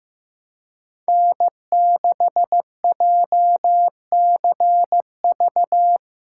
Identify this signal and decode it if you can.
Morse code, heard as a keyed tone, about 15 wpm, 700 Hz. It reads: N6JCV
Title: USA Callsign DE Callsign - 5 Characters - 15wpm